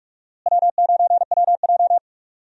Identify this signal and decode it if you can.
Morse code, heard as a keyed tone, about 41 wpm, 700 Hz. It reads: W9WJ
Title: USA Call Signs - 4 Characters - Repeated 3x - 45wpm